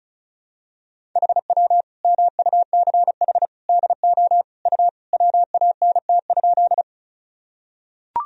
Morse code, heard as a keyed tone, about 36 wpm, 700 Hz. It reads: HW MUCH DO U WANT?
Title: Sentences from Top 200 Words and Abbreviations 35wpm